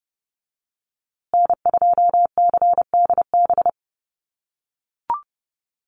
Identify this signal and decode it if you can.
Morse code, heard as a keyed tone, about 28 wpm, 700 Hz. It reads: N2CDB
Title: USA Call Signs Repeated 2x 30wpm